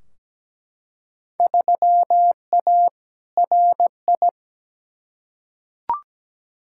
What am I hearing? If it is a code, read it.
Morse code, heard as a keyed tone, about 17 wpm, 700 Hz. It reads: 3A RI
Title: Field Day Exchanges - 17wpm